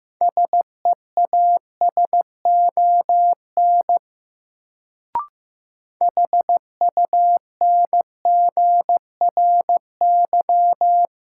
Morse code, heard as a keyed tone, about 15 wpm, 700 Hz. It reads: SEASON HUNGRY
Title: Sentences from Top 1400 Words - New Word Drill Alt - 15wpm